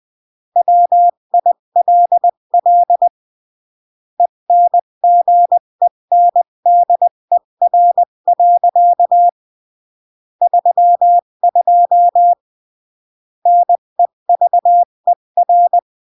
Morse code, heard as a keyed tone, about 20 wpm, 700 Hz. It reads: WILL ENGENDER. 32 NEVER
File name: CW / jodorowsky / 82 commandments 20-15